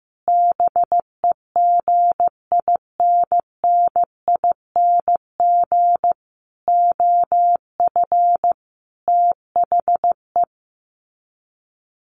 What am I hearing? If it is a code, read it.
Morse code, heard as a keyed tone, about 15 wpm, 700 Hz. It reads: BEGINNING OF THE
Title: Sets of 5 Words 15wpm